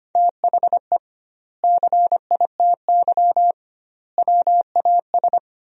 Morse code, heard as a keyed tone, about 25 wpm, 700 Hz. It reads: THE CITY WAS
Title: 2x Word-Spacing - Sentences from Top 500 Words - 25wpm